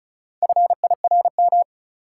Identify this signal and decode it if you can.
Morse code, heard as a keyed tone, about 36 wpm, 700 Hz. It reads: FIRM
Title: Four Letter Words - 35wpm